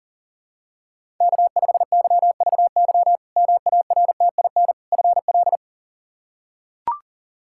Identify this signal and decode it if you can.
Morse code, heard as a keyed tone, about 40 wpm, 700 Hz. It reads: K5YVY MARTIN FL
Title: K1USN SST QSO Practice - 40wpm